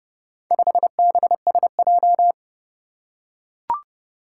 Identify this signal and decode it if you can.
Morse code, heard as a keyed tone, about 25 wpm, 700 Hz. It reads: 5BSJ
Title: Speed-Racing - Four Letter-Number Combinations - 20wpm